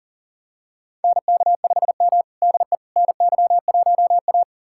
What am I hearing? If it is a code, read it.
Morse code, heard as a keyed tone, about 37 wpm, 700 Hz. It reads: NK5M DE NY1A
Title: USA Callsign DE Callsign - 4 Characters - 40wpm